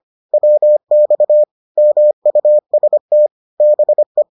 Morse code, heard as a keyed tone, about 25 wpm, 600 Hz. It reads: WX MUST BE
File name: 25wpm-6